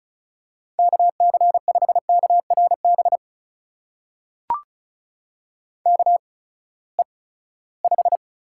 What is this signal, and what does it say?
Morse code, heard as a keyed tone, about 36 wpm, spaced as for 20 wpm, 700 Hz. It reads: KC5KRB KE5
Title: USA Call Signs - 6 Characters - 8x Character Spacing - 35wpm